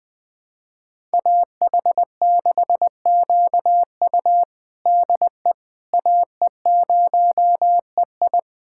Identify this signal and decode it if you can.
Morse code, heard as a keyed tone, about 20 wpm, 700 Hz. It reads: AH6QU DE AE0EI
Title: USA Callsign DE Callsign - 5 Characters - 20wpm